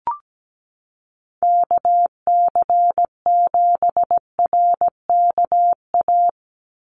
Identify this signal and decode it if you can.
Morse code, heard as a keyed tone, about 17 wpm, 700 Hz. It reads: KC7RKA
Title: USA Call Signs 17wpm